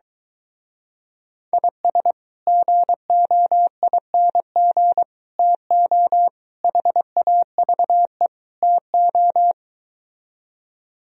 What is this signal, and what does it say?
Morse code, heard as a keyed tone, about 22 wpm, 700 Hz. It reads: IS GOING TO HAVE TO